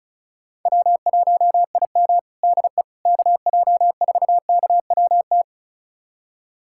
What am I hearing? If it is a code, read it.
Morse code, heard as a keyed tone, about 34 wpm, 700 Hz. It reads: W1IM DE KJ4KWT
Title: USA Callsign DE Callsign 35wpm